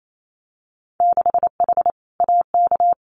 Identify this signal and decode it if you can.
Morse code, heard as a keyed tone, about 28 wpm, 700 Hz. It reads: 6H AK
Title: Winter Field Day Exchanges Encore 28wpm